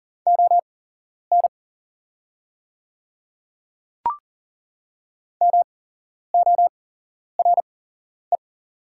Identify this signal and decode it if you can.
Morse code, heard as a keyed tone, about 40 wpm, spaced as for 11 wpm, 690 Hz. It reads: ON MORE
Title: 8x Character Spacing - Top 100 Words - 40wpm